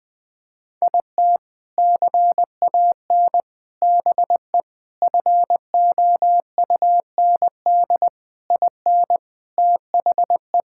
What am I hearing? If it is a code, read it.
Morse code, heard as a keyed tone, about 21 wpm, 700 Hz. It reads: IT CAN BE FOUND IN THE
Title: Sentences with 4-Word Phrases - 20wpm